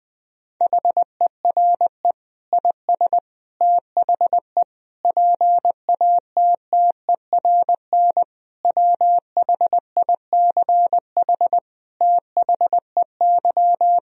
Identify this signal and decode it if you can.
Morse code, heard as a keyed tone, about 20 wpm, 700 Hz. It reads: HERE IS THE PATTERN WHICH THEY
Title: Sentences from Top 1000 Words - Review Part 2 - 20wpm